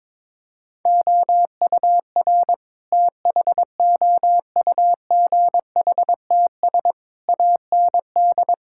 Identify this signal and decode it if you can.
Morse code, heard as a keyed tone, about 22 wpm, 700 Hz. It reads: OUR THOUGHTS AND
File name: practice / 22wpm / Sets of 8 Words - 22wpm